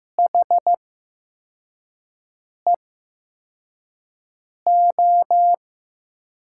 Morse code, heard as a keyed tone, about 15 wpm, spaced as for 4 wpm, 700 Hz. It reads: HEO
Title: USA Call Signs - 6 Characters - 8x Character Spacing - 15wpm